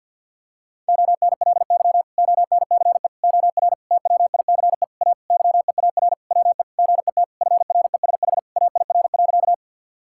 Morse code, heard as a keyed tone, about 50 wpm, 700 Hz. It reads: ONLY ONCE OR TWICE A YEAR WE GET FRESH AIR.
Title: Sentences from Top 1600 Words 50wpm